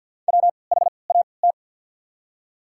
Morse code, heard as a keyed tone, about 49 wpm, spaced as for 28 wpm, 700 Hz. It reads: WHAT
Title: 3x Character Spacing - Top 200 Words - 50wpm